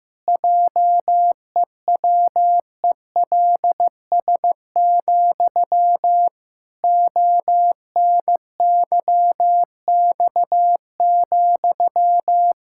Morse code, heard as a keyed tone, about 15 wpm, 700 Hz. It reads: JEWELS, ONYX,
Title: Pangram Challenge v2 - 15wpm